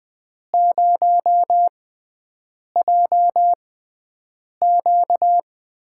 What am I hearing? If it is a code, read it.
Morse code, heard as a keyed tone, about 21 wpm, spaced as for 7 wpm, 700 Hz. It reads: 0JQ
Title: USA Call Signs - 6x Character Spacing - 20wpm